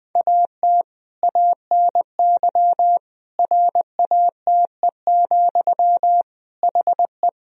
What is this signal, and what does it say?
Morse code, heard as a keyed tone, about 20 wpm, 700 Hz. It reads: AT ANY RATE, HE
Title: Sentences from Top 1000 Words - Review Part 4 - 20wpm